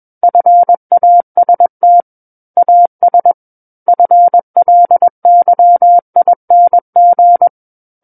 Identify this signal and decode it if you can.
Morse code, heard as a keyed tone, about 21 wpm, 700 Hz. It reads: FAST AS FLYING
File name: Book One / 12-Chapter 12 - 21 WPM